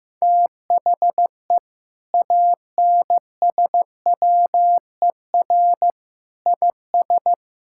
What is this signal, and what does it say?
Morse code, heard as a keyed tone, about 15 wpm, 700 Hz. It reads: THE ANSWER IS
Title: Sentences from Top 1600 Words 15wpm